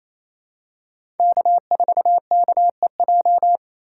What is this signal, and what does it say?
Morse code, heard as a keyed tone, about 28 wpm, 700 Hz. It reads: K4KEJ
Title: USA Call Signs - 5 Characters - 4x Character Spacing - 28wpm